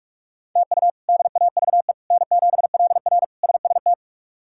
Morse code, heard as a keyed tone, about 43 wpm, 700 Hz. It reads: TU DAVE N7LR SST
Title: K1USN SST QSO Practice - 45wpm